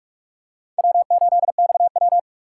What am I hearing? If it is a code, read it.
Morse code, heard as a keyed tone, about 44 wpm, 700 Hz. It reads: W8XW